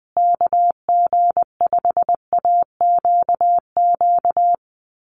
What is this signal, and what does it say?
Morse code, heard as a keyed tone, about 20 wpm, 700 Hz. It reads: KG5AQQ
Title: USA Call Signs Repeated 3x 20wpm